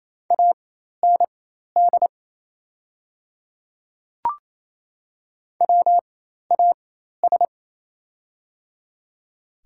Morse code, heard as a keyed tone, about 27 wpm, spaced as for 13 wpm, 690 Hz. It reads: AND WAS
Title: 4x Character Spacing - Top 100 Words - 28wpm